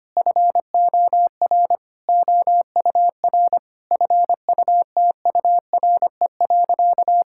Morse code, heard as a keyed tone, about 25 wpm, 700 Hz. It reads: FOR OUR FUTURE.
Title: Sentences from Top 1000 Words - Review Part 6 - 25wpm